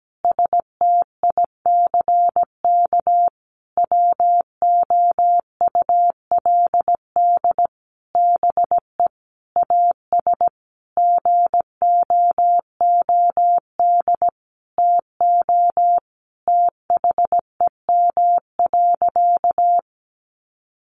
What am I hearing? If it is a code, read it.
Morse code, heard as a keyed tone, about 17 wpm, 700 Hz. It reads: STICK WOULD BE AS GOOD TO THEM.
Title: Sentences from Top 1200 Words 17wpm